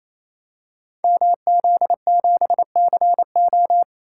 Morse code, heard as a keyed tone, about 28 wpm, 700 Hz. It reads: MZ7CO
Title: Code Groups - 5 Letter:Numbers - 28wpm